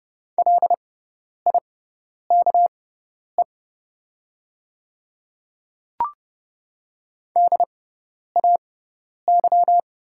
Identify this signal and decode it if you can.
Morse code, heard as a keyed tone, about 30 wpm, spaced as for 10 wpm, 700 Hz. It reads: LIKE DAY
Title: 6x Character Spacing - Top 100 Words - 30wpm